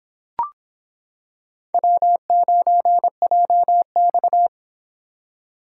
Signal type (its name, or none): Morse code